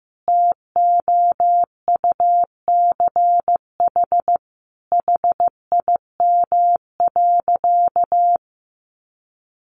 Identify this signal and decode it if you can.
Morse code, heard as a keyed tone, about 15 wpm, 700 Hz. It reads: TOUCH HIM.
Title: Sentences from Top 1300 Words 15wpm